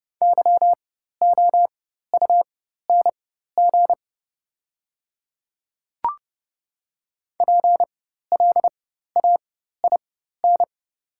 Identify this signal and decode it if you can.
Morse code, heard as a keyed tone, about 30 wpm, spaced as for 14 wpm, 700 Hz. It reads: YOUNG PLAIN
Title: 4x Character Spacing - Top 500 Words - 30wpm